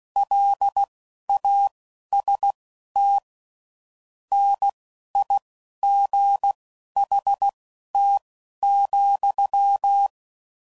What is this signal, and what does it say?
Morse code, heard as a keyed tone, about 16 wpm, spaced as for 11 wpm, 800 Hz. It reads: LAST NIGHT,
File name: CW Code Groups @10WPM/Weather Sets